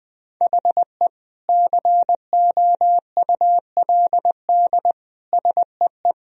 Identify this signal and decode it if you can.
Morse code, heard as a keyed tone, about 21 wpm, 700 Hz. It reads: HE COULD SEE
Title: Sentences from Top 1000 Words - Review Part 3 - 20wpm